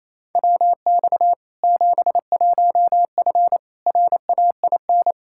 Morse code, heard as a keyed tone, about 28 wpm, 700 Hz. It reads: WX 71F RAIN
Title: K1USN SST QSO Practice 28wpm